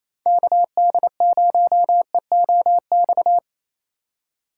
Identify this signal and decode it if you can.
Morse code, heard as a keyed tone, about 33 wpm, 700 Hz. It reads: KD0EOX
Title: Speed Racing - USA Call Signs 28wpm